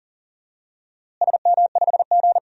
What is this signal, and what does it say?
Morse code, heard as a keyed tone, about 40 wpm, 700 Hz. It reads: SM5G
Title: DXCC-100 Call Signs - 4 Characters - Repeated 2x - 40wpm